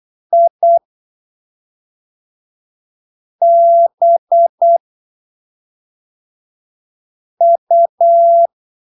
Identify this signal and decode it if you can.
Morse code, heard as a keyed tone, about 8 wpm, 680 Hz. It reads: I B U
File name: arquivos / texto / texto 12 08ppm